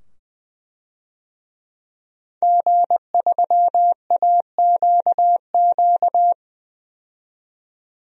Morse code, heard as a keyed tone, about 21 wpm, 700 Hz. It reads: G3AQQ
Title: DXCC-100 Call Signs - 5 Characters - 20wpm